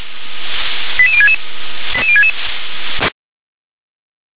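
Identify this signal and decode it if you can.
Five-tone selective call (ZVEI), heard as quick runs of five tones, tones of about 70 ms. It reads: 88066 88066
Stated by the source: ZVEI five-tone call 88185 received on an SDR.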